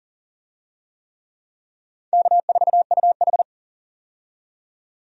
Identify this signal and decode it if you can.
Morse code, heard as a keyed tone, about 40 wpm, 700 Hz. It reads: K4UH
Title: USA Call Signs - 4 Characters - 40wpm